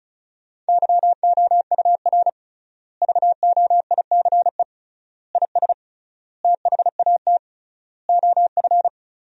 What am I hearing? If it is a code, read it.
Morse code, heard as a keyed tone, about 36 wpm, 700 Hz. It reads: YOUR VOICE IS THAT OF
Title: 3x Word-Spacing - Sentences from Top 500 Words - 35wpm